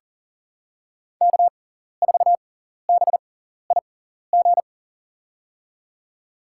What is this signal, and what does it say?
Morse code, heard as a keyed tone, about 40 wpm, spaced as for 14 wpm, 700 Hz. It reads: K4BIG
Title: USA Call Signs - 5 Characters - 6x Character Spacing - 40wpm